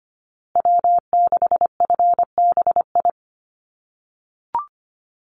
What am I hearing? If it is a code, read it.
Morse code, heard as a keyed tone, about 25 wpm, 700 Hz. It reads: W6FBI
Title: USA Call Signs Repeated 2x 25wpm